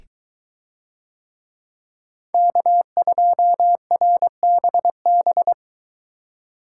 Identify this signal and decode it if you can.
Morse code, heard as a keyed tone, about 23 wpm, 700 Hz. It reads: K2RBB